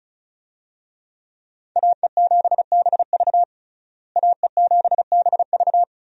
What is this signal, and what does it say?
Morse code, heard as a keyed tone, about 34 wpm, 700 Hz. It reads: AE7BV AE7BV
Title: USA Call Signs - 5 Characters - Repeated 3x - 35wpm